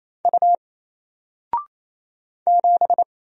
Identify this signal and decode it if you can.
Morse code, heard as a keyed tone, about 27 wpm, 700 Hz. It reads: U 7
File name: Single-Letter-Number-28wpm